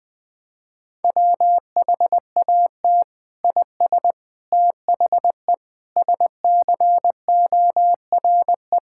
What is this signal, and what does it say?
Morse code, heard as a keyed tone, about 20 wpm, 700 Hz. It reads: WHAT IS THE SCORE
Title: Sentences from Top 1600 Words 20wpm